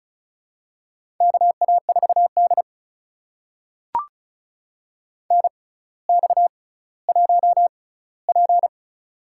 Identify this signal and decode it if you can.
Morse code, heard as a keyed tone, about 36 wpm, spaced as for 18 wpm, 700 Hz. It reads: KA4D NX1P
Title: USA Call Signs - 4 Characters - 6x Character Spacing - 35wpm